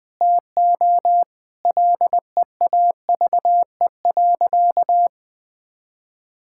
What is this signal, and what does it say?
Morse code, heard as a keyed tone, about 21 wpm, 700 Hz. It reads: TO LEAVE.
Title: Sentences with 5-Word Phrases - 20wpm